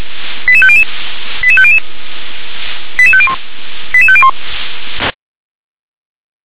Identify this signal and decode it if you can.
Five-tone selective call (ZVEI), heard as quick runs of five tones, tones of about 70 ms. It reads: 88500 88500 88501 88501